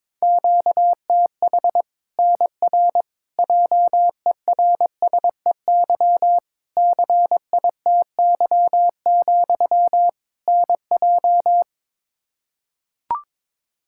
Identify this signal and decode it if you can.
Morse code, heard as a keyed tone, about 22 wpm, 700 Hz. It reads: QTH NR JERSEY CITY, NJ
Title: K1USN SST QSO Practice 22wpm